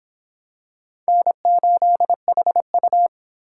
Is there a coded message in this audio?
Yes: Morse code from 1.1 to 3.1 s, heard as a keyed tone, about 25 wpm, 700 Hz. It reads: N8HU